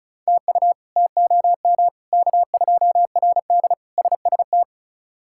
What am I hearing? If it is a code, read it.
Morse code, heard as a keyed tone, about 36 wpm, 700 Hz. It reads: TU TOM K2RD SST
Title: K1USN SST QSO Practice - 35wpm